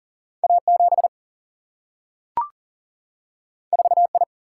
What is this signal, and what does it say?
Morse code, heard as a keyed tone, about 38 wpm, 700 Hz. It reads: A7 4I
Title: Single-Letter Single-Number 40wpm